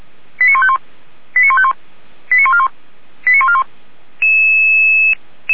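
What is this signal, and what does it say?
Five-tone selective call (ZVEI), heard as quick runs of five tones, tones of about 70 ms. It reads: 89151 89151 89141 89141
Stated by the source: ZVEI five-tone selective call received by SDR.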